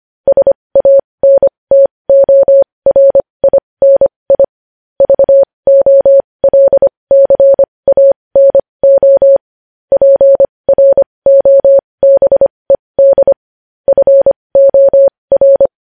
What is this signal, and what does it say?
Morse code, heard as a keyed tone, about 25 wpm, spaced as for 20 wpm, 550 Hz. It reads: SANTORINI VOLCANO PROBED FOR
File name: podcast / News headlines-1x-25x20